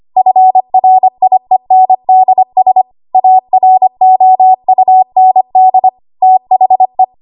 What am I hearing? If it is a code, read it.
Morse code, heard as a keyed tone, about 25 wpm, 750 Hz. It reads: FRIENDS AROUND THE
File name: Archive / 25 WPM / 25WPM